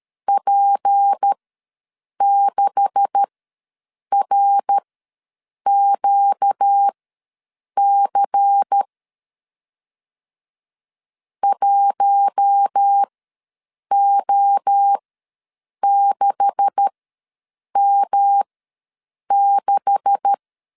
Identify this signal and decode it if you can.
Morse code, heard as a keyed tone, about 13 wpm, spaced as for 7 wpm, 800 Hz. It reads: P6RQC 1O6M6
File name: LEZIONI / 9 286 ws 12wpm 3351